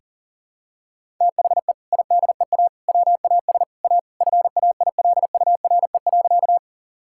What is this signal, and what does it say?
Morse code, heard as a keyed tone, about 40 wpm, 700 Hz. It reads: THE IDEA WAS A FAILURE.
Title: Sentences from Top 1000 Words - Review Part 6 - 40wpm